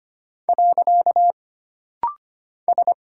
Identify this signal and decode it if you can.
Morse code, heard as a keyed tone, about 25 wpm, 700 Hz. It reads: . S
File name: Single Character (Letter:Number:Punctuation:Prosign) - Rapid-Fire - 25wpm